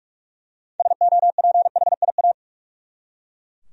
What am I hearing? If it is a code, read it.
Morse code, heard as a keyed tone, about 41 wpm, 700 Hz. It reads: SOPHIA
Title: USA Common Names - 45wpm